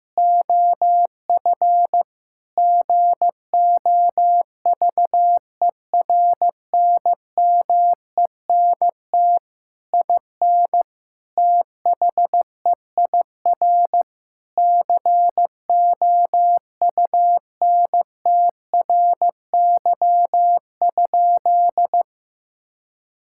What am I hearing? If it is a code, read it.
Morse code, heard as a keyed tone, about 15 wpm, 700 Hz. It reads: OF GOVERNMENT IN THEIR COUNTRY?
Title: Sentences from Top 1000 Words - Review Part 5 - 15wpm